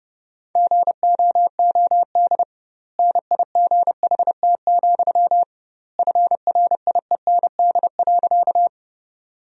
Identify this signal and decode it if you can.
Morse code, heard as a keyed tone, about 32 wpm, 700 Hz. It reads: GOOD NIGHT, FRIEND.
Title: 2x Word-Spacing - Sentences from Top 300 Words - 30wpm